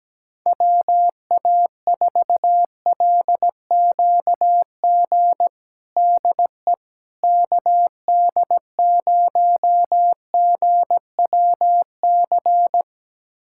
Morse code, heard as a keyed tone, about 17 wpm, 700 Hz. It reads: WA4LQG DE KD0GWC
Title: USA Callsign DE Callsign - 6 Characters - 17wpm